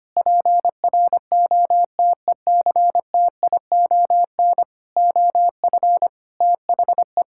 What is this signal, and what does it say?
Morse code, heard as a keyed tone, about 25 wpm, 700 Hz. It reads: PROTECTION OF THE
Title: Sentences from Top 1600 Words 25wpm